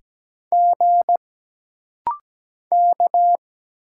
Morse code, heard as a keyed tone, about 17 wpm, 700 Hz. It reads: G K
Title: Single Letter-Number Alt - Rapid-Fire 17wpm